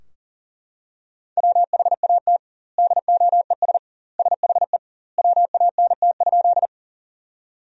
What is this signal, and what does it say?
Morse code, heard as a keyed tone, about 40 wpm, 700 Hz. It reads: WHAT DOES SHE WANT?
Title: 2x Word-Spacing - Sentences from Top 200 Words - 40wpm